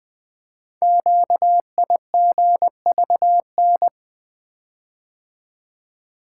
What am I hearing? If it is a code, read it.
Morse code, heard as a keyed tone, about 20 wpm, 700 Hz. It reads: QIGVN
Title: Code Groups - 5 Letters - 20wpm